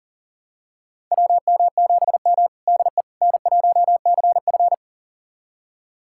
Morse code, heard as a keyed tone, about 40 wpm, 700 Hz. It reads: WM7M DE N1CF
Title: USA Callsign DE Callsign - 4 Characters - 40wpm